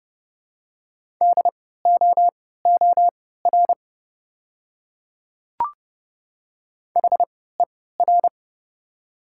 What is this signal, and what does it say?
Morse code, heard as a keyed tone, about 30 wpm, spaced as for 17 wpm, 700 Hz. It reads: DOOR HER
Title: 3x Character Spacing - Top 300 Words - 30wpm